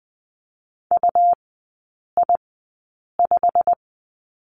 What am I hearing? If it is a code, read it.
Morse code, heard as a keyed tone, about 20 wpm, spaced as for 8 wpm, 700 Hz. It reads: UI5
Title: 133 Lesson 22e - Question Mark - Three Characters 20wpm